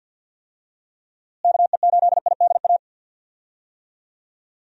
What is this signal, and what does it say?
Morse code, heard as a keyed tone, about 51 wpm, 700 Hz. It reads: KE8IDA
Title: USA Call Signs - 6 Characters - Repeated 3x - 50wpm